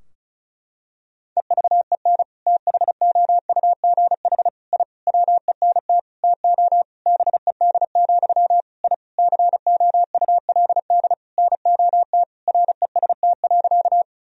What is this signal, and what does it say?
Morse code, heard as a keyed tone, about 34 wpm, 700 Hz. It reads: EVEN THOUGH I WENT TO BED, I COULD NOT REST.
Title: Sentences from Top 1000 Words - Review Part 2 - 35wpm